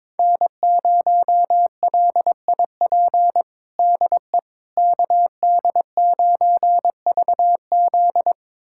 Morse code, heard as a keyed tone, about 22 wpm, 700 Hz. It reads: N0LIP DE KD9VZ
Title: USA Callsign DE Callsign - 5 Characters - 22wpm